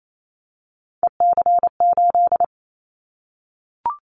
Morse code, heard as a keyed tone, about 28 wpm, 700 Hz. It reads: EC8
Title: Callsign Prefixes for Top 100 DXCC - 28wpm